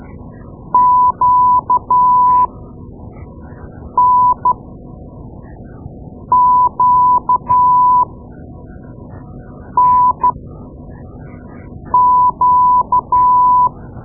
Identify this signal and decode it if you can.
Morse code, heard as a keyed tone, about 13 wpm, spaced as for 4 wpm, 1000 Hz. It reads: QNQNQ
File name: Morse Lessons / Morse Lessons / Tape-2A